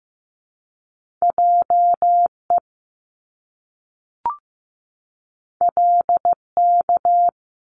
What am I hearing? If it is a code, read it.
Morse code, heard as a keyed tone, about 15 wpm, 700 Hz. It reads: JE LK
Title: Top Two Letter Combinations in Words 15wpm